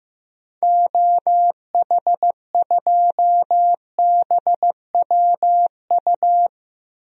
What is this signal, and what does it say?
Morse code, heard as a keyed tone, about 15 wpm, 700 Hz. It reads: OH2BWU
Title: DXCC-100 Call Signs - 6 Characters - Repeated 2x - 15wpm